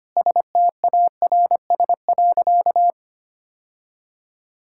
Morse code, heard as a keyed tone, about 25 wpm, 700 Hz. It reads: STARS.
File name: practice / 25wpm / Sentences with 2-Word Phrases - 25wpm